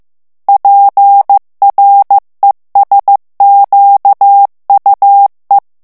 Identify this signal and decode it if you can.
Morse code, heard as a keyed tone, about 15 wpm, 800 Hz. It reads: PRESQUE
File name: morse mp3 / FRANCAIS TEXTE CLAIR 2 VITESSE 14 MOTS MINUTES